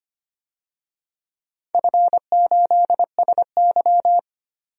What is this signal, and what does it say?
Morse code, heard as a keyed tone, about 25 wpm, 700 Hz. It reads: F8SY
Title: DXCC-100 Call Signs - 4 Characters - 25wpm